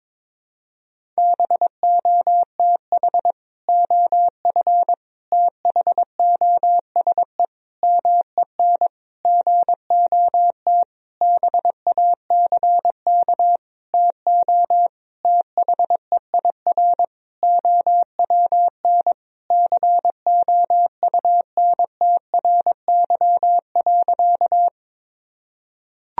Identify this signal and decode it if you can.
Morse code, heard as a keyed tone, about 22 wpm, 700 Hz. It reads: BOTH OF THOSE MEN GOT BACK TO THEIR OWN COUNTRY.
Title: Sentences from Top 1000 Words - Review Part 1 - 22wpm